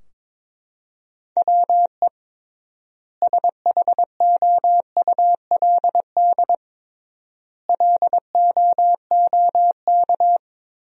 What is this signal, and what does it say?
Morse code, heard as a keyed tone, about 22 wpm, 700 Hz. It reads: WE SHOULD LOOK
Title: 3x Word-Spacing - Sentences from Top 300 Words - 22wpm